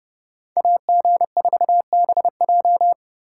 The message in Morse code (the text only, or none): AG4BJ